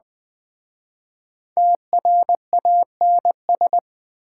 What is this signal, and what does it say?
Morse code, heard as a keyed tone, about 21 wpm, 700 Hz. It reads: TRANS
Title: Speed-Racing Prefixes and Suffixes 17wpm